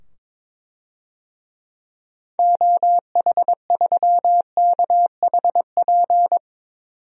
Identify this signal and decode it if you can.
Morse code, heard as a keyed tone, about 21 wpm, 700 Hz. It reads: OH3KHP
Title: DXCC-100 Call Signs - 6 Characters - 22wpm